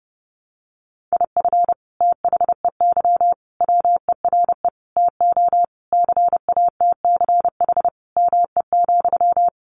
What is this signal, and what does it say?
Morse code, heard as a keyed tone, about 28 wpm, 700 Hz. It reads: IF THEY WERE TO CATCH ME,
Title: Sentences from Top 1500 Words 30wpm